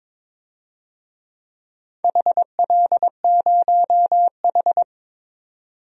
Morse code, heard as a keyed tone, about 22 wpm, 700 Hz. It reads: HL0H
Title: DXCC-100 Call Signs - 4 Characters - 22wpm